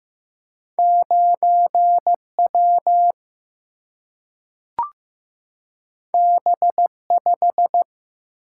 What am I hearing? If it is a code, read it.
Morse code, heard as a keyed tone, about 15 wpm, 700 Hz. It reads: 9 W B 5
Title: Single-Letter Single-Number 15wpm